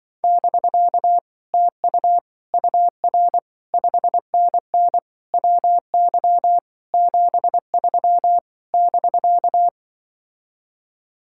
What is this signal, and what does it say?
Morse code, heard as a keyed tone, about 24 wpm, 700 Hz. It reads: <BK> TU UR 5NN WY 73 <BK>
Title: Speed-Racing Common QSO Phrases Encore - 20wpm